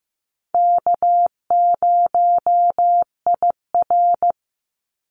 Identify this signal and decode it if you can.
Morse code, heard as a keyed tone, about 15 wpm, 700 Hz. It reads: K0IR
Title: Easy Call Signs Repeated 2x - 15wpm